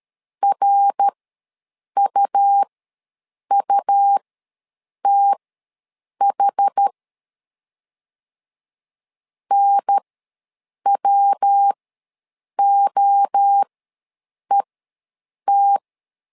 Morse code, heard as a keyed tone, about 13 wpm, spaced as for 7 wpm, 800 Hz. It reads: RUUTH NWOET
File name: LEZIONI / 3 MWRO / 3 MWRO ws 12wpm 1914